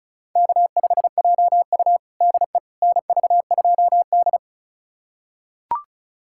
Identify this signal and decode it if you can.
Morse code, heard as a keyed tone, about 36 wpm, 700 Hz. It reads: K5JU DE NV2D
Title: USA Callsign DE Callsign - 4 Characters - 35wpm